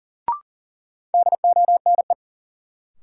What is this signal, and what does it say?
Morse code, heard as a keyed tone, about 40 wpm, 700 Hz. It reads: DONE
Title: Sets of 5 Words Encore - New Word Drill - 40wpm